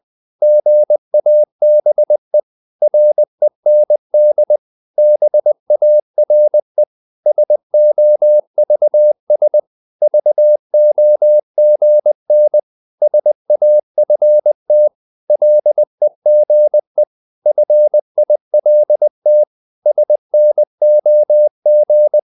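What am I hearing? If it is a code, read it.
Morse code, heard as a keyed tone, about 20 wpm, 600 Hz. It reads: GABE REND BARE SOVS VOGN SAFT LEGE FILT SNOG
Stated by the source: Morse code practice of short Danish words at 20 wpm, from CW med Gnister.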